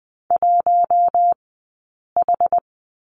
Morse code, heard as a keyed tone, about 20 wpm, 700 Hz. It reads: 1 H
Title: 060 Lesson 11e - Letter h - Three Characters 20wpm